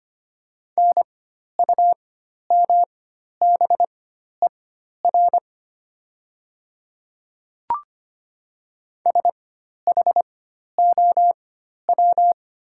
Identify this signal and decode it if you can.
Morse code, heard as a keyed tone, about 25 wpm, spaced as for 12 wpm, 700 Hz. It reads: NUMBER SHOW
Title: 4x Character Spacing - Top 200 Words - 25wpm